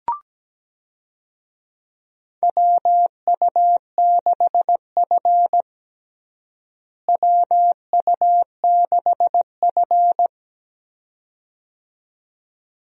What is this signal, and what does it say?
Morse code, heard as a keyed tone, about 17 wpm, 700 Hz. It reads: WU6F WU6F
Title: USA Call Signs - 4 Characters - Repeated 2x - 17wpm